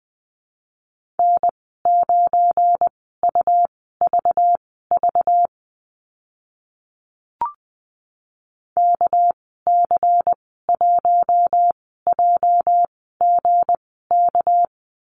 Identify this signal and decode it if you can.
Morse code, heard as a keyed tone, about 20 wpm, spaced as for 14 wpm, 700 Hz. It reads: N9UVV KC1JGK
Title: 213 Lesson 34g - Number 8 - Call Signs 20wpm